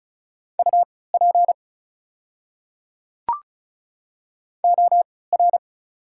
Morse code, heard as a keyed tone, about 34 wpm, spaced as for 20 wpm, 690 Hz. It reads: UP OR
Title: 3x Character Spacing - Top 300 Words - 35wpm